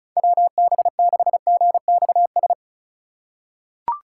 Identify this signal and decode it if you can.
Morse code, heard as a keyed tone, about 34 wpm, 700 Hz. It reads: WB6GXS
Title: USA Call Signs - 6 Characters - 2x Character Spacing - 35wpm